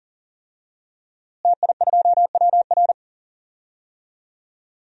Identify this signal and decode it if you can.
Morse code, heard as a keyed tone, about 40 wpm, 700 Hz. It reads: TI2WR
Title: DXCC-100 Call Signs - 5 Characters - Repeated 3x - 40wpm